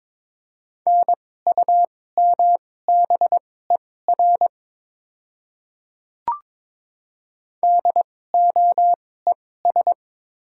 Morse code, heard as a keyed tone, about 22 wpm, spaced as for 16 wpm, 700 Hz. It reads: NUMBER DOES
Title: 2x Character Spacing - Top 200 Words - 22wpm